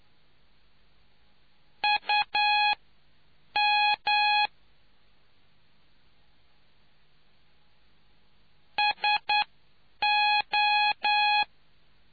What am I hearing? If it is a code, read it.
Morse code, heard as a keyed tone, about 9 wpm, spaced as for 7 wpm, 840 Hz. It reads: UM SO